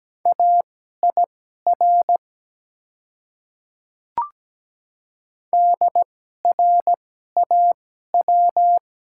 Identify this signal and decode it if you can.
Morse code, heard as a keyed tone, about 17 wpm, spaced as for 12 wpm, 700 Hz. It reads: AIR DRAW
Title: 2x Character Spacing - Top 300 Words - 17wpm